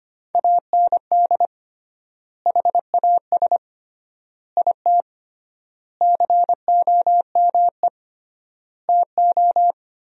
Morse code, heard as a keyed tone, about 25 wpm, 700 Hz. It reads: AND HAS IT COME TO
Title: 3x Word-Spacing - Sentences from Top 100 Words - 25wpm